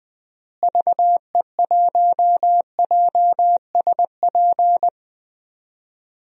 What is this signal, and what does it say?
Morse code, heard as a keyed tone, about 20 wpm, 700 Hz. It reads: VE1JSP
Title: DXCC-100 Call Signs - 6 Characters - Repeated 2x - 20wpm